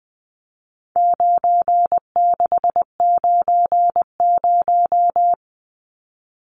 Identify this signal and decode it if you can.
Morse code, heard as a keyed tone, about 20 wpm, 700 Hz. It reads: 9690
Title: Numbers - 4-Digit 20wpm